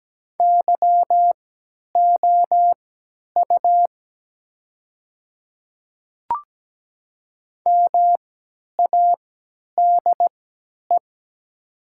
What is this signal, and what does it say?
Morse code, heard as a keyed tone, about 17 wpm, spaced as for 10 wpm, 700 Hz. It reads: YOU MADE
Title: 3x Character Spacing - Top 200 Words - 17wpm